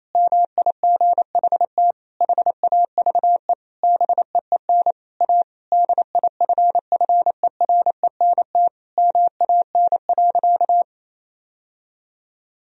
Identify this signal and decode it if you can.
Morse code, heard as a keyed tone, about 28 wpm, 700 Hz. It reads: MIGHT HAVE BEEN A DIFFERENT MAN.
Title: Sentences from Top 1000 Words - Review Part 3 - 28wpm